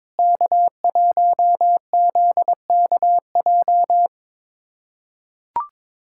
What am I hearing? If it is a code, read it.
Morse code, heard as a keyed tone, about 21 wpm, 700 Hz. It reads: K1ZKJ
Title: USA Callsign DE Callsign 22wpm